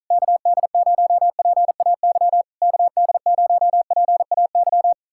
Morse code, heard as a keyed tone, about 46 wpm, 700 Hz. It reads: KD0PAY KD0PAY